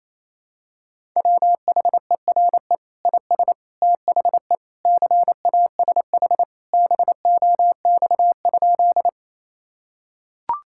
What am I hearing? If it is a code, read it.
Morse code, heard as a keyed tone, about 28 wpm, 700 Hz. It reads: WHERE IS THE CASH BOX?
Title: Sentences from Top 1000 Words - Review Part 5 - 28wpm